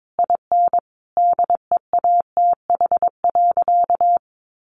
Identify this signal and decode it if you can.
Morse code, heard as a keyed tone, about 22 wpm, 700 Hz. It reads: IN DEATH.
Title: Sentences from Top 1300 Words 22wpm